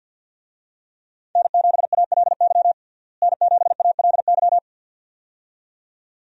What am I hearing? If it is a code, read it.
Morse code, heard as a keyed tone, about 51 wpm, 700 Hz. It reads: N7ALY N7ALY
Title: USA Call Signs - 5 Characters - Repeated 2x - 50wpm